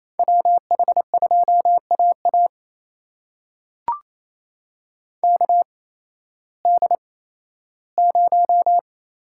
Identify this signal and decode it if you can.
Morse code, heard as a keyed tone, about 28 wpm, spaced as for 15 wpm, 700 Hz. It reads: WH2AA KD0
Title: USA Call Signs - 8x Character Spacing - 28wpm